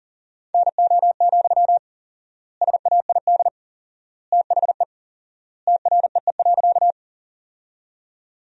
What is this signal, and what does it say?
Morse code, heard as a keyed tone, about 40 wpm, 700 Hz. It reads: NO, SAID THE TREE.
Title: 4x Word-Spacing - Sentences from Top 300 Words - 40wpm